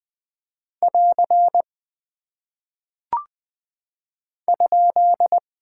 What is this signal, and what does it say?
Morse code, heard as a keyed tone, about 20 wpm, 700 Hz. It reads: <AR> ?
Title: Single Punctuation-Prosign - 20wpm